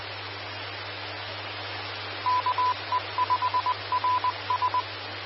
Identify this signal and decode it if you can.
Morse code, heard as a keyed tone, about 20 wpm, 1000 Hz. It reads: KE5RS
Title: Austin, TX Amateur Repeaters